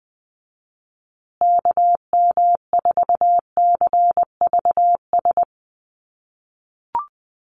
Morse code, heard as a keyed tone, about 20 wpm, 700 Hz. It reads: KM4CVS
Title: USA Call Signs Repeated 3x 20wpm